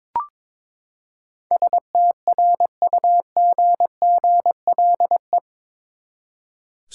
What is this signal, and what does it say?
Morse code, heard as a keyed tone, about 22 wpm, 700 Hz. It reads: STRUGGLE
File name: practice / 22wpm / Eight Letter Words - 22wpm